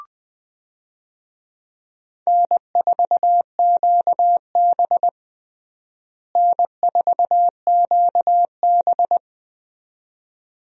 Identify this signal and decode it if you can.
Morse code, heard as a keyed tone, about 20 wpm, 700 Hz. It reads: N4QB N4QB
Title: USA Call Signs - 4 Characters - Repeated 2x - 20wpm